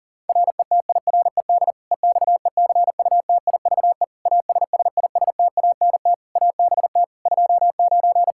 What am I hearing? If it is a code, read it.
Morse code, heard as a keyed tone, about 40 wpm, 700 Hz. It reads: RETIRED EXECUTIVE ASSISTANT ABT 29
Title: Practice QSO - Exchange 3 - More Info - 40wpm